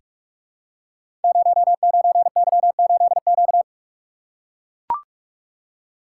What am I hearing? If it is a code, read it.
Morse code, heard as a keyed tone, about 44 wpm, 690 Hz. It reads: 09Y8Q
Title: Code Groups - 5 Letter:Numbers - 45wpm